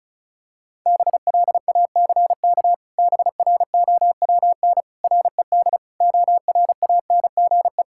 Morse code, heard as a keyed tone, about 34 wpm, 700 Hz. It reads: BLACK BROWN RED ORANGE
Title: Pangram Challenge v2 - 35wpm